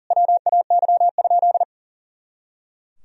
Morse code, heard as a keyed tone, about 40 wpm, 700 Hz. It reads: WAY?
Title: Sentences from Top 200 Words and Abbreviations 40wpm